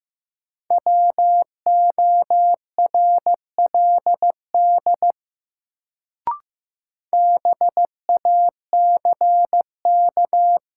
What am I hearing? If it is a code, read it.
Morse code, heard as a keyed tone, about 15 wpm, 700 Hz. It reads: WORLD BACK
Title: Sets of 2 Words - New Word Drill - 15wpm